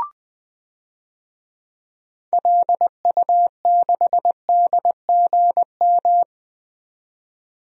Morse code, heard as a keyed tone, about 20 wpm, 700 Hz. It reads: LU6DGM
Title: DXCC-100 Call Signs - 6 Characters - 20wpm